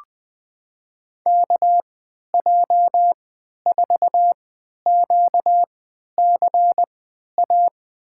Morse code, heard as a keyed tone, about 20 wpm, spaced as for 11 wpm, 700 Hz. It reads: KJ4QCA
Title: USA Call Signs - 3x Character Spacing - 20wpm